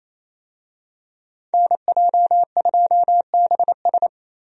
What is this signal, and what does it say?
Morse code, heard as a keyed tone, about 27 wpm, 700 Hz. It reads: NJ2BS
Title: USA Call Signs - 5 Characters - Repeated 3x - 28wpm